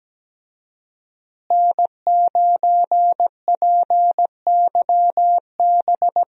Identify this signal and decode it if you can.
Morse code, heard as a keyed tone, about 17 wpm, 700 Hz. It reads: N9PYB
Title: USA Call Signs - 5 Characters - Repeated 3x - 17wpm